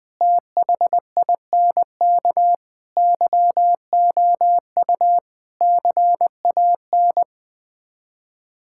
Morse code, heard as a keyed tone, about 21 wpm, 700 Hz. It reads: THINK YOU CAN
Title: Build A Phrase - 5-Word Phrases - 20wpm